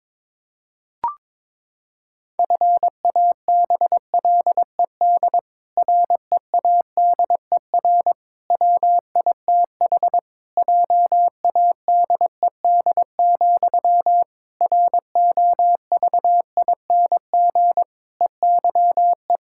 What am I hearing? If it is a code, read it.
Morse code, heard as a keyed tone, about 22 wpm, 700 Hz. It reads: FABLED READER WITH JADED, ROVING EYE
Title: Pangram Challenge v2 - 22wpm